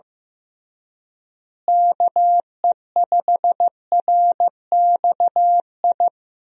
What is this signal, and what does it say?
Morse code, heard as a keyed tone, about 15 wpm, 700 Hz. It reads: KE5RXI
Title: USA Call Signs - 6 Characters - Repeated 3x - 15wpm